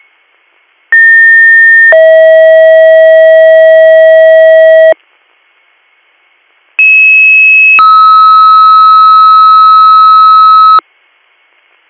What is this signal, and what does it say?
Two-tone page, heard as a one-second tone, then a three-second tone, about 1800 then 640 Hz, about 2600 then 1290 Hz. It is heard from 0.9 to 10.8 s.